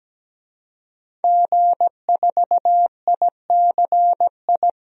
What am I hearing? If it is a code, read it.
Morse code, heard as a keyed tone, about 17 wpm, 700 Hz. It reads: G4ICI
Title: DXCC-100 Call Signs - 5 Characters - 17wpm